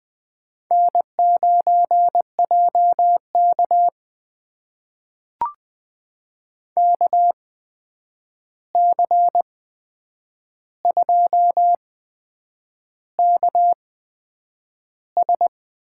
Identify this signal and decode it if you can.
Morse code, heard as a keyed tone, about 20 wpm, spaced as for 8 wpm, 700 Hz. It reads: N9JK KC2KS
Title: USA Call Signs - 8x Character Spacing - 20wpm